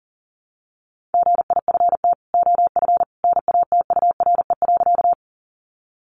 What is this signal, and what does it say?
Morse code, heard as a keyed tone, about 40 wpm, 700 Hz. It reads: GIFT OF NATURE.
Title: Sentences from Top 900 Words 40wpm